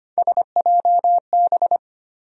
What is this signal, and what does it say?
Morse code, heard as a keyed tone, about 25 wpm, 700 Hz. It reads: SJB
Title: USA Call Sign Postfixes - 3 Letters 25wpm